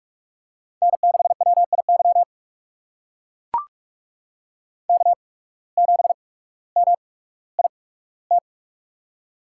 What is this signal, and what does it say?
Morse code, heard as a keyed tone, about 44 wpm, spaced as for 19 wpm, 700 Hz. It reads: N6WIY K7MIT
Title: USA Call Signs - 5 Characters - 8x Character Spacing - 45wpm